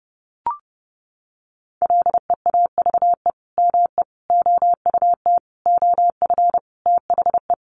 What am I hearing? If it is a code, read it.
Morse code, heard as a keyed tone, about 30 wpm, 700 Hz. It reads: LEAVE ME OUT OF THE
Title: Sentences from Top 400 Words 30wpm